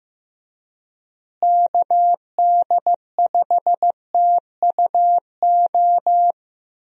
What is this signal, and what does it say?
Morse code, heard as a keyed tone, about 15 wpm, 700 Hz. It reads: KD5TUO
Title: USA Call Signs - 6 Characters - Repeated 2x - 15wpm